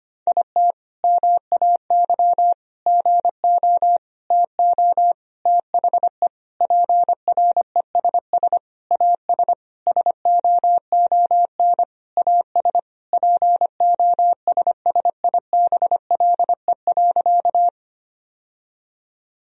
Morse code, heard as a keyed tone, about 25 wpm, 700 Hz. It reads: IT MAY GO TO THE PRESS AS SOON AS POSSIBLE.
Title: Sentences from Top 1000 Words - Review Part 1 - 25wpm